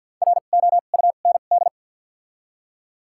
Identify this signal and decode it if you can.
Morse code, heard as a keyed tone, about 48 wpm, spaced as for 35 wpm, 700 Hz. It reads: ROUND
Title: 2x Character Spacing - Top 500 Words - 50wpm